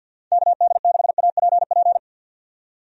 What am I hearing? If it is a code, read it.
Morse code, heard as a keyed tone, about 52 wpm, 700 Hz. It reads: KD6APP
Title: USA Call Signs - 6 Characters - 50wpm